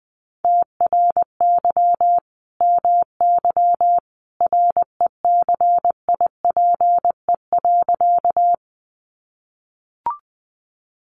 Morse code, heard as a keyed tone, about 20 wpm, 700 Hz. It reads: TRY MY RECIPE.
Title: Sentences from Top 1200 Words 20wpm